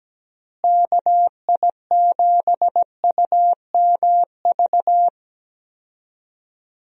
Morse code, heard as a keyed tone, about 17 wpm, 700 Hz. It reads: KI7UMV
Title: USA Call Signs - 6 Characters - Repeated 3x - 17wpm